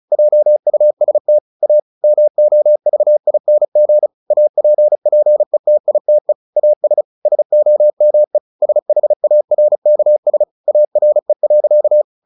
Morse code, heard as a keyed tone, about 34 wpm, 600 Hz. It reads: JUST A MOVING APPETITE AS SOME SHARKS ARE.
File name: the old man and the sea 2 35wpm